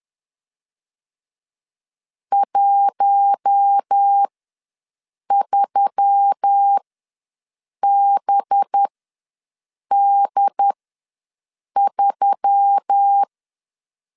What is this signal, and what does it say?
Morse code, heard as a keyed tone, about 10 wpm, spaced as for 6 wpm, 800 Hz. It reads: 13BD3
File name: LEZIONI / 5 LBVC / 5 LB ws 10wpm 3616